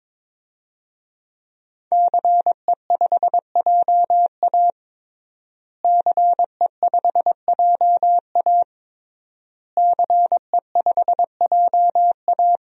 Morse code, heard as a keyed tone, about 22 wpm, 700 Hz. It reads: CE5JA CE5JA CE5JA
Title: DXCC-100 Call Signs - 5 Characters - Repeated 3x - 22wpm